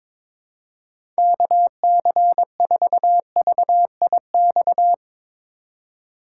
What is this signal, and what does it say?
Morse code, heard as a keyed tone, about 22 wpm, 700 Hz. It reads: KC4VIX
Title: USA Call Signs - 6 Characters - 4x Character Spacing - 22wpm